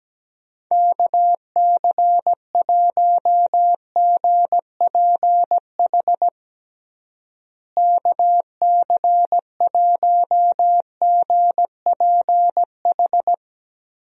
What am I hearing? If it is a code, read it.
Morse code, heard as a keyed tone, about 17 wpm, 700 Hz. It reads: KC1GPH KC1GPH
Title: USA Call Signs - 6 Characters - Repeated 3x - 17wpm